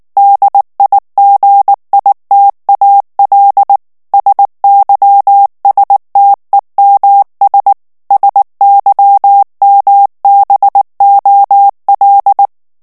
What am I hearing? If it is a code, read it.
Morse code, heard as a keyed tone, about 19 wpm, 800 Hz. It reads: DIGITAL SYSTEMS SYMBOL
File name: ANGLAIS texte clair 3 vitesse 18 mots minute